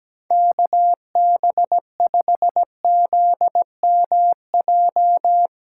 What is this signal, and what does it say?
Morse code, heard as a keyed tone, about 17 wpm, 700 Hz. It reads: KB5ZMJ
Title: USA Callsign DE Callsign - 6 Characters - 17wpm